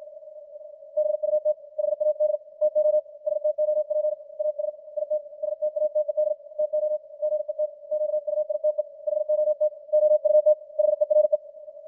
Morse code, heard as a keyed tone, about 56 wpm, 610 Hz. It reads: BUT HAD TO STOP AS IT STATED TO GET QUITE HOT OUT HERE
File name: archived / CWBros